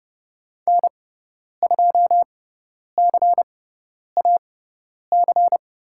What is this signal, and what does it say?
Morse code, heard as a keyed tone, about 30 wpm, spaced as for 10 wpm, 700 Hz. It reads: N2CAC
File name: Lesson 18g - Letter m - Call Signs 30wpm